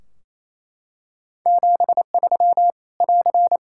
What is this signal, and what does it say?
Morse code, heard as a keyed tone, about 28 wpm, 700 Hz. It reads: 73 <AR>
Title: Practice QSO - Exchange 4 - Ending - 28wpm